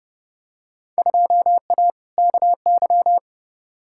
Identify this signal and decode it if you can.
Morse code, heard as a keyed tone, about 32 wpm, 700 Hz. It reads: 2A KY
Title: Field Day Exchanges - 30wpm